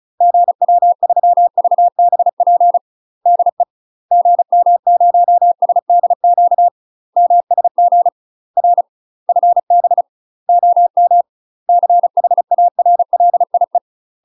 Morse code, heard as a keyed tone, about 36 wpm, 700 Hz. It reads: GW3VBP DE GM0SDQ MSG R FB OM CHARLIE
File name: UK MASTER QSOs TRAINING / 35WPM / UK MASTER QSOs 0013